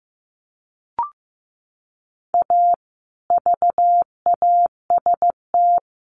Morse code, heard as a keyed tone, about 15 wpm, 700 Hz. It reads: A VAST
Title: Sentences from Top 1300 Words 15wpm